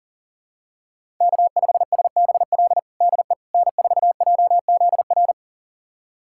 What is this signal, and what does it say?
Morse code, heard as a keyed tone, about 40 wpm, 700 Hz. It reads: K5SBL DE N4JZR
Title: USA Callsign DE Callsign - 5 Characters - 40wpm